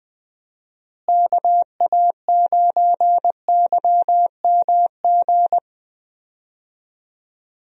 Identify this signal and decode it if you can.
Morse code, heard as a keyed tone, about 21 wpm, 700 Hz. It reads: KA9YMG
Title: USA Call Signs - 6 Characters - Repeated 3x - 20wpm